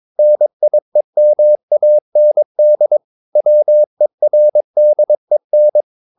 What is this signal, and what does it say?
Morse code, heard as a keyed tone, about 22 wpm, 600 Hz. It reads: NIEMAND WERDEN